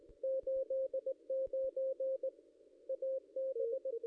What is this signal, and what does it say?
Morse code, heard as a keyed tone, about 19 wpm, 530 Hz, with radio noise. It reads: 89 AB
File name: contests / ss-cw-k5tr-op-n5rz -1600